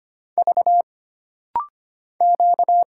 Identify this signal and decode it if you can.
Morse code, heard as a keyed tone, about 25 wpm, 700 Hz. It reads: V Q
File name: Single Character (Letter:Number:Punctuation:Prosign) - Rapid-Fire - 25wpm